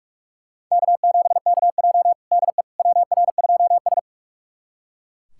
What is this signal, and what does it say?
Morse code, heard as a keyed tone, about 41 wpm, 700 Hz. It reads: K7KJ DE WR2S
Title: USA Callsign DE Callsign - 4 Characters - 45wpm